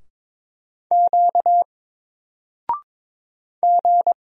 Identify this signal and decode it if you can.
Morse code, heard as a keyed tone, about 22 wpm, 700 Hz. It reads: Q G
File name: Single-Letter-Number-22wpm